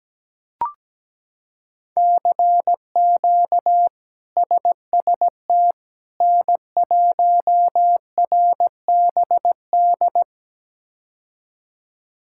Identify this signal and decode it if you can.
Morse code, heard as a keyed tone, about 17 wpm, 700 Hz. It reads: CQ SST N1RBD
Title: K1USN SST QSO Practice - 17wpm